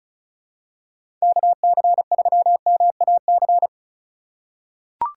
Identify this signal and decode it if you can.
Morse code, heard as a keyed tone, about 36 wpm, 700 Hz. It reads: KC3MAC
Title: USA Call Signs - 6 Characters - 4x Character Spacing - 35wpm